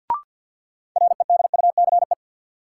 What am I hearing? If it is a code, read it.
Morse code, heard as a keyed tone, about 53 wpm, 700 Hz. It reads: REDUCE
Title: Sentences from Top 1100 Words - New Word Drill - 50wpm